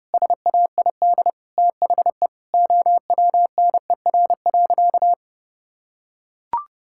Morse code, heard as a keyed tone, about 32 wpm, 700 Hz. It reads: SAID THE OWNER.
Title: Sentences from Top 1600 Words 30wpm